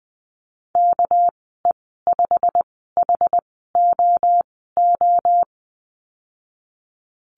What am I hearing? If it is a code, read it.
Morse code, heard as a keyed tone, about 20 wpm, spaced as for 15 wpm, 700 Hz. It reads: KE5HOO
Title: 200 Lesson 32g - Letter k - Call Signs 20wpm